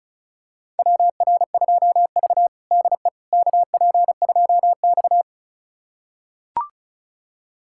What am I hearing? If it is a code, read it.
Morse code, heard as a keyed tone, about 36 wpm, 700 Hz. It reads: WR2V DE KP2X
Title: USA Callsign DE Callsign - 4 Characters - 35wpm